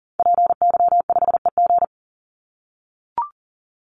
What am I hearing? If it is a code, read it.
Morse code, heard as a keyed tone, about 40 wpm, 700 Hz. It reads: PY5EG
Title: Call Signs - 40wpm